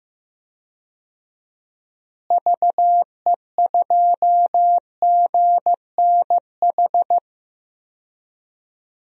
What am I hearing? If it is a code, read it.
Morse code, heard as a keyed tone, about 15 wpm, 700 Hz. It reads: VE2GNH
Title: DXCC-100 Call Signs - 6 Characters - Repeated 3x - 15wpm